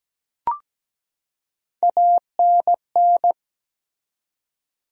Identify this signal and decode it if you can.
Morse code, heard as a keyed tone, about 17 wpm, 700 Hz. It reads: ANN
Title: USA Common Names - Max 3 Letters - 17wpm